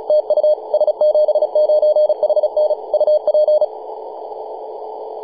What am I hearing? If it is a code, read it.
Morse code, heard as a keyed tone, about 34 wpm, 610 Hz, with radio noise. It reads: TU S79HN UP
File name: s79hn-24-910mhz-cw